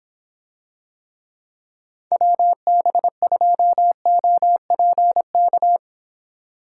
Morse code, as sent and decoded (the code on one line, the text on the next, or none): .-- -... ..--- --- .--. -.-
WB2OPK